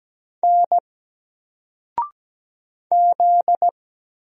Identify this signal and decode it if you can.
Morse code, heard as a keyed tone, about 17 wpm, 700 Hz. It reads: N Z
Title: Single Letter-Number Alt 17wpm